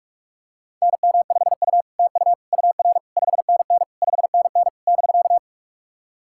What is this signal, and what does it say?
Morse code, heard as a keyed tone, about 44 wpm, 700 Hz. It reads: NM5U TU UR 5NN 5NN <BK>
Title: SOTA QSO Practice 45wpm